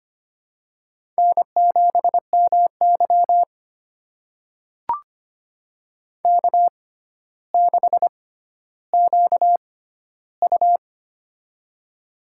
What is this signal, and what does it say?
Morse code, heard as a keyed tone, about 25 wpm, spaced as for 13 wpm, 700 Hz. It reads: N7MY K6QU
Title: USA Call Signs - 4 Characters - 6x Character Spacing - 25wpm